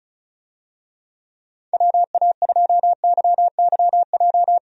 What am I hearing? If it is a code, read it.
Morse code, heard as a keyed tone, about 36 wpm, 700 Hz. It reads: WA2YYJ
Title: USA Call Signs - 6 Characters - 35wpm